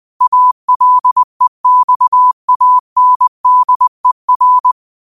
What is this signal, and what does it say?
Morse code, heard as a keyed tone, about 20 wpm, 1000 Hz. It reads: ALEXANDER
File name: qotdfiles / 20wpm